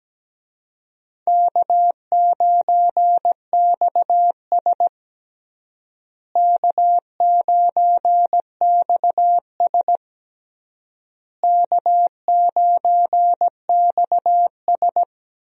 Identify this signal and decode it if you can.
Morse code, heard as a keyed tone, about 17 wpm, 700 Hz. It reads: K9XS K9XS K9XS
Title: USA Call Signs - 4 Characters - Repeated 3x - 17wpm